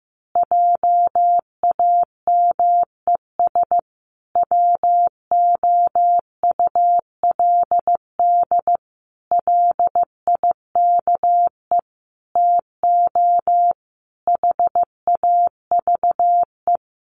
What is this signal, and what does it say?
Morse code, heard as a keyed tone, about 14 wpm, 700 Hz. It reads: JAMES WOULD LIKE TO HAVE
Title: Easy Phrases 15wpm